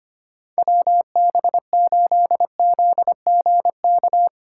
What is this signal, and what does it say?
Morse code, heard as a keyed tone, about 31 wpm, 700 Hz. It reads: WB8ZGK